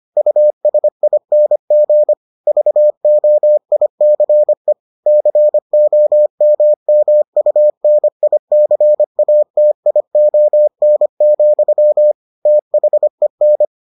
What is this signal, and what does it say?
Morse code, heard as a keyed tone, about 25 wpm, 600 Hz. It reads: USING VOICE COMMUNICATION, THEN